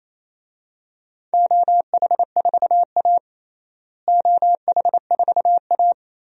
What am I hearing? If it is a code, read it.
Morse code, heard as a keyed tone, about 28 wpm, 700 Hz. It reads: OH4A OH4A
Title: DXCC-100 Call Signs - 4 Characters - Repeated 3x - 28wpm